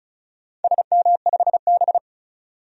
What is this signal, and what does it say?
Morse code, heard as a keyed tone, about 34 wpm, 700 Hz. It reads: SM5B
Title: DXCC-100 Call Signs - 4 Characters - 35wpm